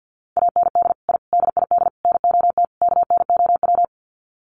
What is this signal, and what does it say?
Morse code, heard as a keyed tone, about 50 wpm, 700 Hz. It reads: AND I DID NOT KNOW
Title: Sets of 5 Words Encore 50wpm